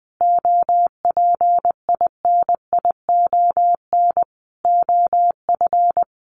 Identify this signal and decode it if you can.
Morse code, heard as a keyed tone, about 20 wpm, 700 Hz. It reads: OPINION OF
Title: Sentences from Top 1300 Words 20wpm